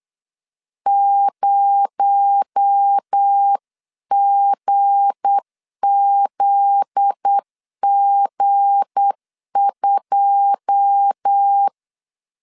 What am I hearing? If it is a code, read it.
Morse code, heard as a keyed tone, about 8 wpm, 800 Hz. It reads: 0GZG2
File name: LEZIONI / 9 2867 8wpm 3440